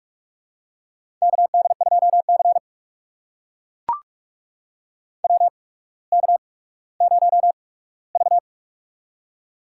Morse code, heard as a keyed tone, about 42 wpm, spaced as for 19 wpm, 700 Hz. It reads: KD2C WK0V
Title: USA Call Signs - 4 Characters - 8x Character Spacing - 45wpm